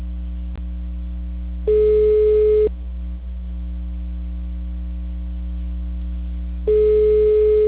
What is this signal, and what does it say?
Ringback tone (Europe, 425 Hz).